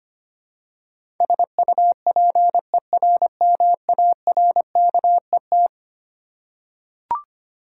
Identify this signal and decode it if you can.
Morse code, heard as a keyed tone, about 25 wpm, 700 Hz. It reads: SUPERMARKET
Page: morse-code-ninja-practice-25wpm